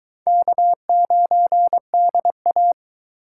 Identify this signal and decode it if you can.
Morse code, heard as a keyed tone, about 24 wpm, 700 Hz. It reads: K9DA